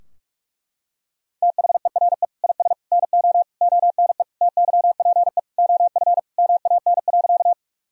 Morse code, heard as a keyed tone, about 41 wpm, 700 Hz. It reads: THERE IS NO ONE TYPE OF MAN.
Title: Sentences from Top 1000 Words - Review Part 3 - 45wpm